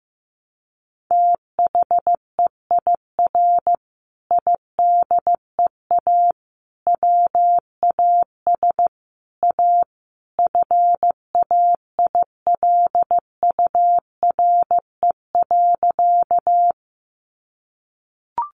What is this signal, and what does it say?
Morse code, heard as a keyed tone, about 15 wpm, 700 Hz. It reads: THEIR IDEA WAS A FAILURE.
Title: Sentences from Top 900 Words 15wpm